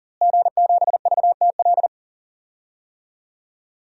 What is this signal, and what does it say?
Morse code, heard as a keyed tone, about 40 wpm, 700 Hz. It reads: G7VTL
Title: DXCC-100 Call Signs - 5 Characters - Repeated 3x - 40wpm